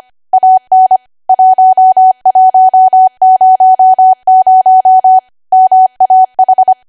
Morse code, heard as a keyed tone, about 25 wpm, 750 Hz. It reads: AN 1100 MAH
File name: Archive / 25 WPM / 25WPM